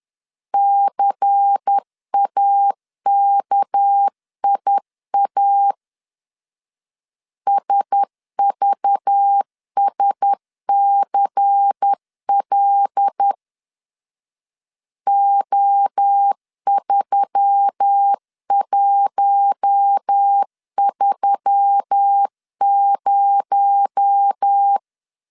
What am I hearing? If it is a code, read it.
Morse code, heard as a keyed tone, about 10 wpm, 800 Hz. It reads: CAKIA SVSCL O3130
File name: CW / DET / 1 6 1500C S B 10wpm 4137